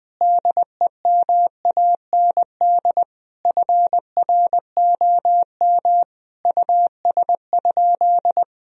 Morse code, heard as a keyed tone, about 20 wpm, 700 Hz. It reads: DEMAND FROM US?
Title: Sentences from Top 1000 Words - Review Part 4 - 20wpm